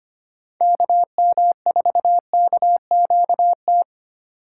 Morse code, heard as a keyed tone, about 33 wpm, 700 Hz. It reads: KM4KQT